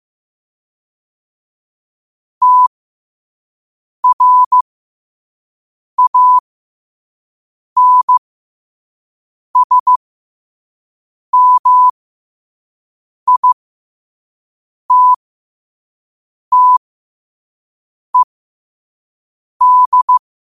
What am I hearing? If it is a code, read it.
Morse code, heard as a keyed tone, about 15 wpm, spaced as for 5 wpm, 1000 Hz. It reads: TRANSMITTED
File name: qotdfiles / 5wpm